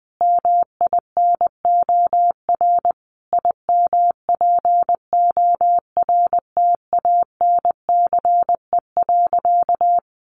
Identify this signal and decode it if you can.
Morse code, heard as a keyed tone, about 20 wpm, 700 Hz. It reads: MINOR IMPORTANCE.
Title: Sentences from Top 1500 Words 20wpm